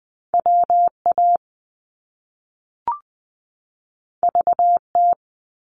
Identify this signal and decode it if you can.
Morse code, heard as a keyed tone, about 20 wpm, 700 Hz. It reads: WA VT
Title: US States 20wpm